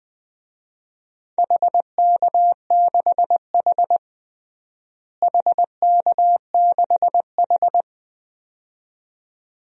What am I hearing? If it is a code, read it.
Morse code, heard as a keyed tone, about 21 wpm, 700 Hz. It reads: HK6H HK6H
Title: DXCC-100 Call Signs - 4 Characters - Repeated 2x - 20wpm